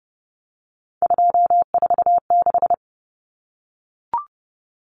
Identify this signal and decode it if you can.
Morse code, heard as a keyed tone, about 30 wpm, 700 Hz. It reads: 246
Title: Numbers - 3-Digit 30wpm